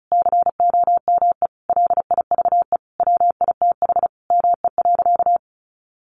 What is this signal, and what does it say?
Morse code, heard as a keyed tone, about 34 wpm, 700 Hz. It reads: COME LIVE WITH ME.
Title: Sentences from Top 1300 Words 35wpm